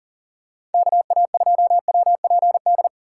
Morse code, heard as a keyed tone, about 40 wpm, 700 Hz. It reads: KA2WPD
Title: USA Call Signs - 6x Character Spacing - 40wpm